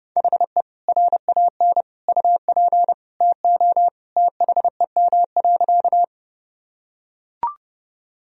Morse code, heard as a keyed tone, about 32 wpm, 700 Hz. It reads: HE RAN UP TO THEM.
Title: Sentences from Top 1000 Words - Review Part 3 - 30wpm